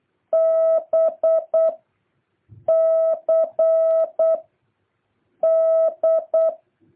Morse code, heard as a keyed tone, about 8 wpm, 650 Hz. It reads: B C D
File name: jh8cbh / 2023cw